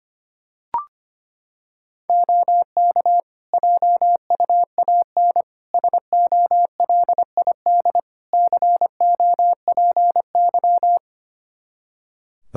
Morse code, heard as a keyed tone, about 25 wpm, 700 Hz. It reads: OK JUAN SOLID COPY
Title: K1USN SST QSO Practice 25wpm